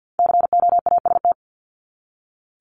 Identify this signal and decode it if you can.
Morse code, heard as a keyed tone, about 48 wpm, 700 Hz. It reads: COAST
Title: Top 501-750 Words 50wpm